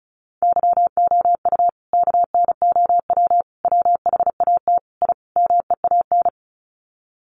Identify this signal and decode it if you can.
Morse code, heard as a keyed tone, about 34 wpm, 700 Hz. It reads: YOU KNOW WHAT I MEAN
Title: Sets of 5 Words 35wpm